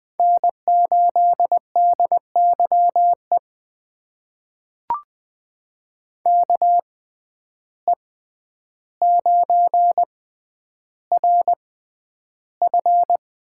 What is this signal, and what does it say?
Morse code, heard as a keyed tone, about 20 wpm, spaced as for 10 wpm, 700 Hz. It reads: N8DYE KE9RF
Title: USA Call Signs - 5 Characters - 6x Character Spacing - 20wpm